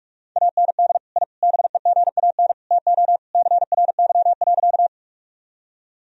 Morse code, heard as a keyed tone, about 42 wpm, 700 Hz. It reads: AND I BEGAN TO CRY.
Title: Sentences from Top 1000 Words - Review Part 2 - 45wpm